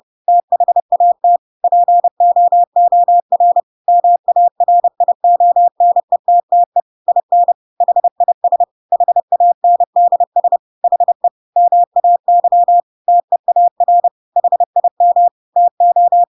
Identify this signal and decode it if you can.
Morse code, heard as a keyed tone, about 30 wpm, 700 Hz. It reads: THAT POOR MARIONETTE IN HIS HANDS HE MAY TEAR HIM TO
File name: Pinocchio / Pinocchio ch3 30 30 ik1hge